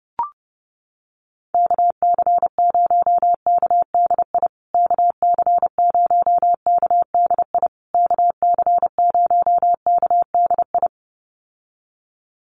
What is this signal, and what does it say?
Morse code, heard as a keyed tone, about 29 wpm, 700 Hz. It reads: KC0KDI KC0KDI KC0KDI
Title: USA Call Signs Repeated 3x 30wpm